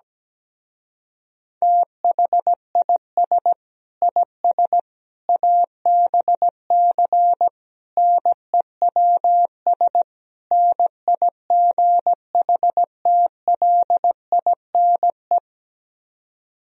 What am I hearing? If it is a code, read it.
Morse code, heard as a keyed tone, about 17 wpm, 700 Hz. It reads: THIS IS ABC NEWS NIGHTLINE
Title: Speed-Racing - Sets of 5 Words 17wpm